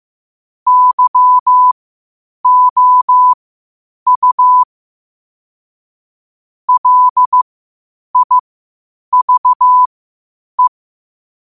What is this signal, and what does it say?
Morse code, heard as a keyed tone, about 15 wpm, spaced as for 8 wpm, 1000 Hz. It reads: YOU LIVE
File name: qotdfiles / 8wpm